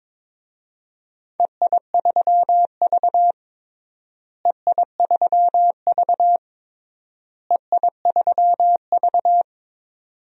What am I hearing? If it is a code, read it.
Morse code, heard as a keyed tone, about 21 wpm, 700 Hz. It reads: EI3V EI3V EI3V
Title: DXCC-100 Call Signs - 4 Characters - Repeated 3x - 22wpm